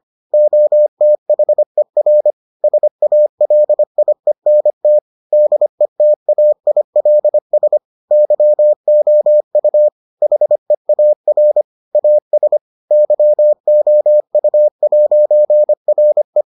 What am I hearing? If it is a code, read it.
Morse code, heard as a keyed tone, about 25 wpm, 600 Hz. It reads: OTHER SALIENT DETAILS YOU HEAR AS YOU'RE